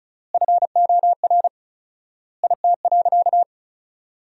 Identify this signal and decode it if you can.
Morse code, heard as a keyed tone, about 34 wpm, 700 Hz. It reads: FOR IT.
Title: 4x Word-Spacing - Sentences from Top 300 Words - 35wpm